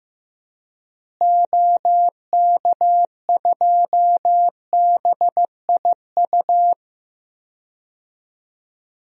Morse code, heard as a keyed tone, about 15 wpm, 700 Hz. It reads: OK2BIU
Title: DXCC-100 Call Signs - 6 Characters - Repeated 2x - 15wpm